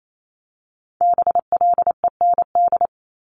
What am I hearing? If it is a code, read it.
Morse code, heard as a keyed tone, about 28 wpm, 700 Hz. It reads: BLEND
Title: Multi-letter Phonemes 28wpm